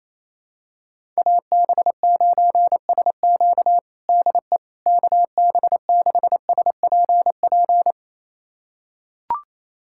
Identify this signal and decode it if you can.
Morse code, heard as a keyed tone, about 28 wpm, 700 Hz. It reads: AB9SQ DE KB6SPP
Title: USA Callsign DE Callsign 28wpm